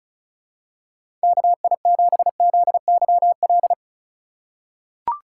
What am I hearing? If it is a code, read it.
Morse code, heard as a keyed tone, about 34 wpm, 700 Hz. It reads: KI7ZYL
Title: USA Call Signs - 2x Character Spacing - 35wpm